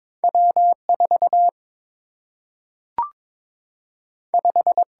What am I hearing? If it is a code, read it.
Morse code, heard as a keyed tone, about 22 wpm, 700 Hz. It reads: W 4 5
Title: Single-Letter Single-Number 22wpm